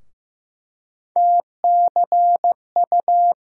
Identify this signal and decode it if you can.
Morse code, heard as a keyed tone, about 15 wpm, 700 Hz. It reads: TCU
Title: USA Call Sign Postfixes - 3 Letters 15wpm